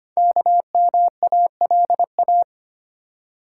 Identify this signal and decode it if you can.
Morse code, heard as a keyed tone, about 25 wpm, 700 Hz. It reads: KMALA
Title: Code Groups - 5 Letters - 25wpm